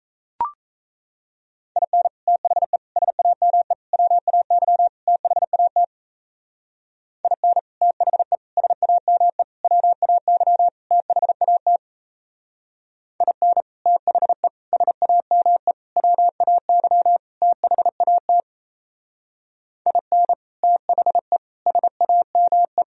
Morse code, heard as a keyed tone, about 37 wpm, 700 Hz. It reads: IN THE SAME WAY THAT IN THE SAME WAY THAT IN THE SAME WAY THAT IN THE SAME